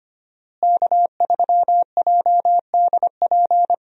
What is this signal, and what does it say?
Morse code, heard as a keyed tone, about 25 wpm, 700 Hz. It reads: K3JDP
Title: USA Call Signs - 5 Characters - Repeated 2x - 25wpm